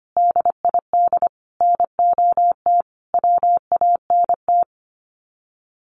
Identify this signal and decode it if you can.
Morse code, heard as a keyed tone, about 25 wpm, 700 Hz. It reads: DID NOT WANT
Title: Sets of 5 Words Encore 25wpm